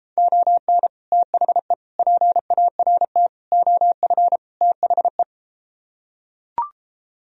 Morse code, heard as a keyed tone, about 33 wpm, 700 Hz. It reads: ON THE PART OF THE